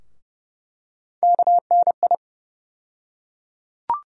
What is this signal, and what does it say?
Morse code, heard as a keyed tone, about 30 wpm, 700 Hz. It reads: KNI
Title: USA Call Sign Postfixes - 3 Letters 30wpm